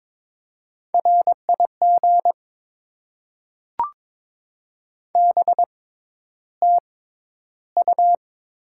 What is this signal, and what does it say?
Morse code, heard as a keyed tone, about 22 wpm, spaced as for 11 wpm, 700 Hz. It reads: RIG BTU
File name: Top 20 QSO Elements - 6x Character Spacing - 22wpm